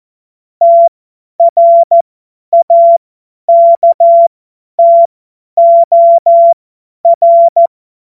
Morse code, heard as a keyed tone, about 14 wpm, spaced as for 10 wpm, 680 Hz. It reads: TRAKTOR
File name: Morsekurs / Witze / Witz 04. April 2023 50 F70